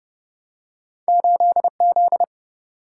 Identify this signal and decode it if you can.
Morse code, heard as a keyed tone, about 30 wpm, 700 Hz. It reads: 8Z
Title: Single-Letter Single-Number 30wpm